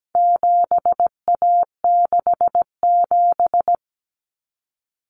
Morse code, heard as a keyed tone, about 17 wpm, 700 Hz. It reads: 7A67
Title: Common Cut Numbers - 4-Digit 17wpm